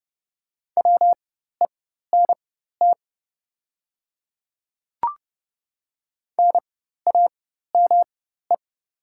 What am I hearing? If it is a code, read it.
Morse code, heard as a keyed tone, about 30 wpm, spaced as for 14 wpm, 700 Hz. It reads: WENT NAME
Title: 4x Character Spacing - Top 300 Words - 30wpm